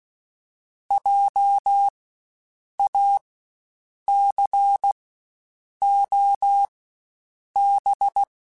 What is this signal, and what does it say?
Morse code, heard as a keyed tone, about 16 wpm, spaced as for 7 wpm, 800 Hz. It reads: JACOB
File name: Spaced Alphas / Spaced Alphas mp3s / Js sp @15